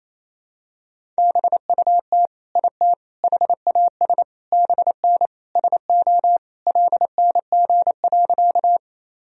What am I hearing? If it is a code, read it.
Morse code, heard as a keyed tone, about 28 wpm, 700 Hz. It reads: BUT IT HAS BN SO LNG.
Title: Sentences from Top 100 Words and Abbreviations 28wpm